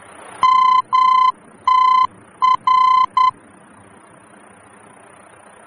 Morse code, heard as a keyed tone, about 10 wpm, 1020 Hz. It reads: MTR
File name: germany / mx / MTR-D-VOR